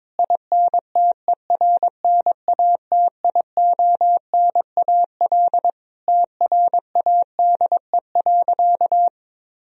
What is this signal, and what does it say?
Morse code, heard as a keyed tone, about 22 wpm, 700 Hz. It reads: INTERNATIONAL TRADE.
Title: Sentences from Top 1000 Words - Review Part 4 - 22wpm